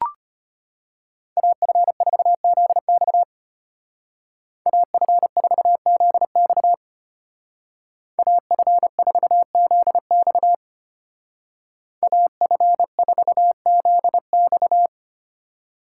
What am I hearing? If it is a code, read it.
Morse code, heard as a keyed tone, about 32 wpm, 700 Hz. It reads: AF4ZX AF4ZX AF4ZX AF4ZX